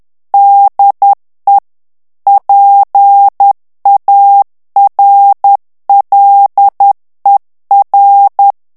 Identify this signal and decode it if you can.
Morse code, heard as a keyed tone, about 10 wpm, 800 Hz. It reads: DE PARLER
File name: morse mp3 / FRANCAIS TEXTE CLAIR 2 VITESSE 10 MOTS MINUTES